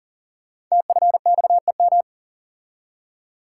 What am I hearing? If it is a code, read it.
Morse code, heard as a keyed tone, about 42 wpm, 690 Hz. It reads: TFXEM
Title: Code Groups - 5 Letters - 40wpm